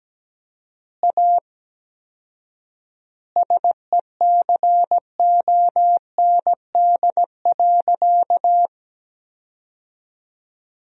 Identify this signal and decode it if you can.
Morse code, heard as a keyed tone, about 17 wpm, 700 Hz. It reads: A SECOND.
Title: 4x Word-Spacing - Sentences from Top 500 Words - 17wpm